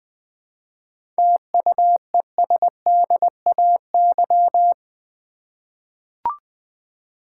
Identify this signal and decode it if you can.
Morse code, heard as a keyed tone, about 20 wpm, 700 Hz. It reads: TUESDAY
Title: 7 Letter Words - First 3 Letters Given - 20wpm